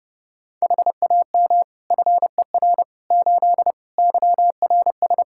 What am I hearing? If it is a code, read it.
Morse code, heard as a keyed tone, about 32 wpm, 700 Hz. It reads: HAM FER 8 YRS
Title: Practice QSO - Exchange 3 - More Info - 30wpm